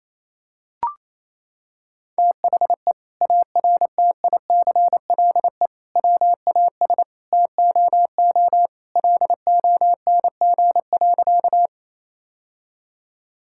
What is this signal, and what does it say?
Morse code, heard as a keyed tone, about 28 wpm, 700 Hz. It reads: THE ARTICLE WAS TOO LONG.
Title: Sentences from Top 1000 Words - Review Part 5 - 28wpm